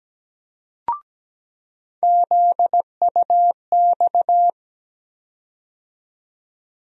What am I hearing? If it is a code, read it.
Morse code, heard as a keyed tone, about 17 wpm, 700 Hz. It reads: ZUX
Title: USA Call Sign Postfixes - 3 Letters 17wpm